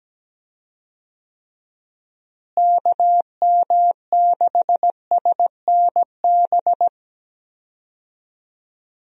Morse code, heard as a keyed tone, about 17 wpm, 700 Hz. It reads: KM6SNB
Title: USA Call Signs - 6 Characters - Repeated 2x - 17wpm